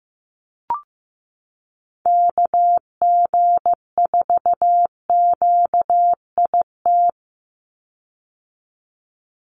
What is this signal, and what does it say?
Morse code, heard as a keyed tone, about 15 wpm, 700 Hz. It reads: KG4QIT
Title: USA Call Signs 15wpm